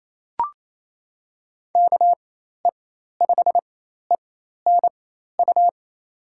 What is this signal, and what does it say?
Morse code, heard as a keyed tone, about 28 wpm, spaced as for 13 wpm, 700 Hz. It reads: KE5ENU
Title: USA Call Signs - 6 Characters - 4x Character Spacing - 28wpm